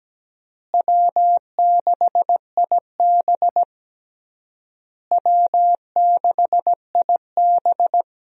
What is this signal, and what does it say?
Morse code, heard as a keyed tone, about 17 wpm, 700 Hz. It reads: W6IB W6IB
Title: USA Call Signs - 4 Characters - Repeated 3x - 17wpm